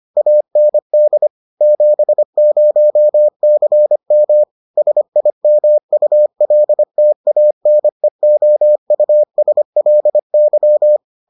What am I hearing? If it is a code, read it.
Morse code, heard as a keyed tone, about 25 wpm, 600 Hz. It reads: AND 70CM SIMULTANEOUSLY